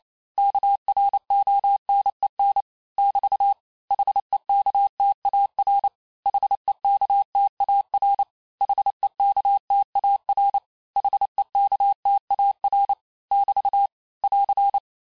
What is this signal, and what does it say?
Morse code, heard as a keyed tone, about 28 wpm, 800 Hz. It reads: KRONEN <BT> HEKTAR HEKTAR HEKTAR HEKTAR <BT> <AR>
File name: cours-du-mercredi / audios / cours-865-teste-les-mots-vitesse-27mpm